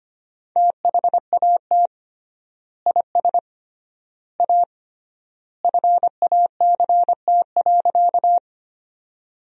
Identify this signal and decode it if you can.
Morse code, heard as a keyed tone, about 25 wpm, 700 Hz. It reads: THAT IS A FACT.
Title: 3x Word-Spacing - Sentences from Top 500 Words - 25wpm